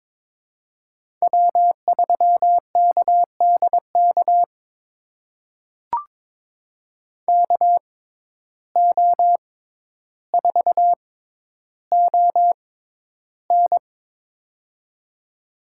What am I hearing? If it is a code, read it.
Morse code, heard as a keyed tone, about 22 wpm, spaced as for 11 wpm, 700 Hz. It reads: W3KDK KO4ON
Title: USA Call Signs - 5 Characters - 6x Character Spacing - 22wpm